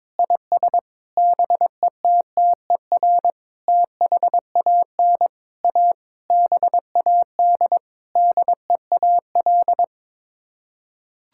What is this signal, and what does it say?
Morse code, heard as a keyed tone, about 22 wpm, 700 Hz. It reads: IS BETTER THAN A BAD DEAL
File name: Sets of 9 Words - 22wpm